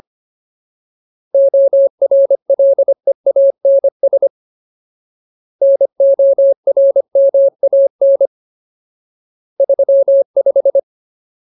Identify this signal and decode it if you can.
Morse code, heard as a keyed tone, about 25 wpm, 550 Hz. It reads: ORLEANS NORMAN 35
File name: Ultimate Word List / 25WPM Ultimate